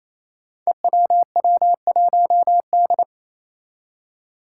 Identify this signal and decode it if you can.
Morse code, heard as a keyed tone, about 27 wpm, 700 Hz. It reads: EWW1D
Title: Five Challenging Letter-Number Combinations 28wpm